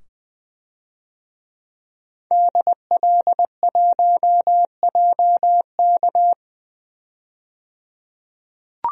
Morse code, heard as a keyed tone, about 20 wpm, 700 Hz. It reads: DL1JK
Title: DXCC-100 Call Signs - 5 Characters - 20wpm